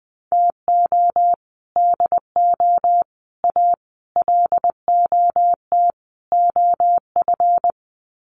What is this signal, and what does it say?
Morse code, heard as a keyed tone, about 20 wpm, 700 Hz. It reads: TO DO A LOT OF
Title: Sets of 5 Words Encore 20wpm